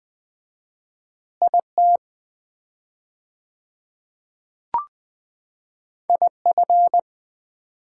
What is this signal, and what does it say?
Morse code, heard as a keyed tone, about 20 wpm, 700 Hz. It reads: IT IF
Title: 4x Word-Spacing - Sets of 3 Words - 20wpm